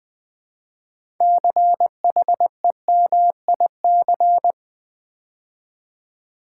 Morse code, heard as a keyed tone, about 20 wpm, 700 Hz. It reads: CHEMIC
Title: Build A Word - 3 Letters At A Time - 8-Letter Words - 20wpm